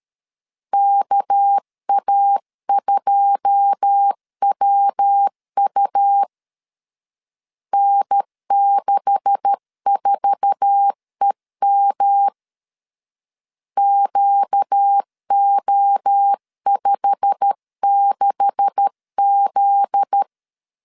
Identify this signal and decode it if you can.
Morse code, heard as a keyed tone, about 13 wpm, 800 Hz. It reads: KA2WU N64EM QO56Z
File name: LEZIONI / 9 286 12wpm 2305